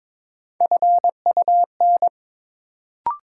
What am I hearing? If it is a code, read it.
Morse code, heard as a keyed tone, about 21 wpm, 700 Hz. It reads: FUN
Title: Sentences from Top 700 Words - New Word Drill - 22wpm